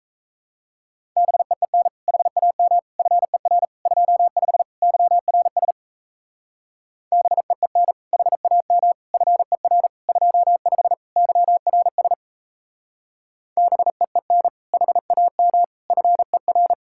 Morse code, heard as a keyed tone, about 33 wpm, 700 Hz. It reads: BEEN HAM FER 25 YRS BEEN HAM FER 25 YRS BEEN HAM FER
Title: Speed-Racing Common QSO Phrases - 28wpm